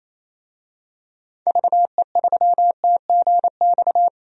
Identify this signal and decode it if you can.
Morse code, heard as a keyed tone, about 28 wpm, 700 Hz. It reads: VE3TGX
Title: DXCC-100 Call Signs - 6 Characters - 28wpm